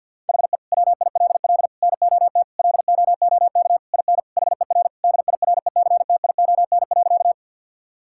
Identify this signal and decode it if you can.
Morse code, heard as a keyed tone, about 51 wpm, 700 Hz. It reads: HE WILL NOT LOOK IN HER DIRECTION.
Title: Sentences from Top 1000 Words - Review Part 5 - 50wpm